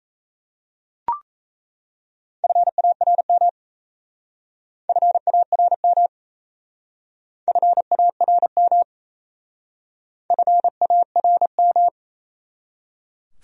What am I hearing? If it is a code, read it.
Morse code, heard as a keyed tone, about 37 wpm, 700 Hz. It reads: FARM FARM FARM FARM